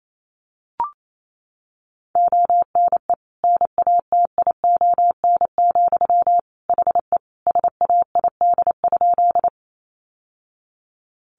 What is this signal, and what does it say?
Morse code, heard as a keyed tone, about 28 wpm, 700 Hz. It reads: ONE NATION, HE SAID?
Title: Sentences from Top 900 Words 28wpm